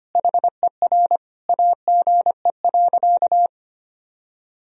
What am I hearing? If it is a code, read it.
Morse code, heard as a keyed tone, about 25 wpm, 700 Hz. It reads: HER AGE.
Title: Sentences from Top 1000 Words - Review Part 3 - 25wpm